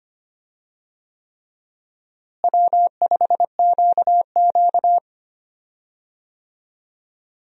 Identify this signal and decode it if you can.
Morse code, heard as a keyed tone, about 25 wpm, 700 Hz. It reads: W5QQ
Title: USA Call Signs - 4 Characters - Repeated 3x - 25wpm